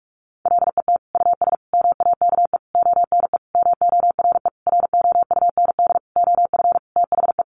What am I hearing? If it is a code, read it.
Morse code, heard as a keyed tone, about 44 wpm, 700 Hz. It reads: LET US MAKE ONE MORE ROUND OF THE
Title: Sentences from Top 700 Words 45wpm